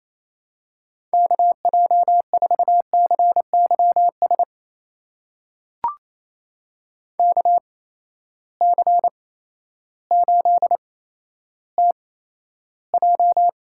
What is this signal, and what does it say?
Morse code, heard as a keyed tone, about 28 wpm, spaced as for 13 wpm, 700 Hz. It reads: KJ4CYS KC8TJ
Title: USA Call Signs - 6 Characters - 8x Character Spacing - 28wpm